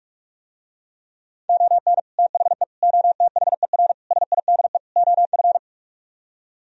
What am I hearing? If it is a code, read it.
Morse code, heard as a keyed tone, about 44 wpm, 700 Hz. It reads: ON THE OTHER SIDE OF